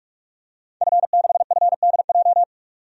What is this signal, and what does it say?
Morse code, heard as a keyed tone, about 46 wpm, 700 Hz. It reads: F6FDJ
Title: DXCC-100 Call Signs - 5 Characters - 45wpm